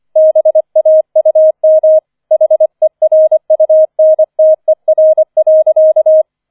Morse code, heard as a keyed tone, about 25 wpm, 630 Hz. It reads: BAUM HERUNTER.